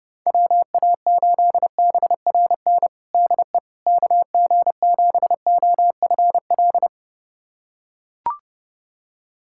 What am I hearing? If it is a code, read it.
Morse code, heard as a keyed tone, about 30 wpm, 700 Hz. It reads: WA8BRN DE KG7OFL
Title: USA Callsign DE Callsign - 6 Characters - 30wpm